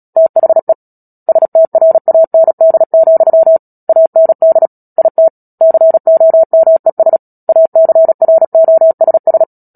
Morse code, heard as a keyed tone, about 37 wpm, 650 Hz. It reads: THE STRAND, AND IT COMES ACROSS
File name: Book Two / 10-Chapter 10 - Epilogue - 36 WPM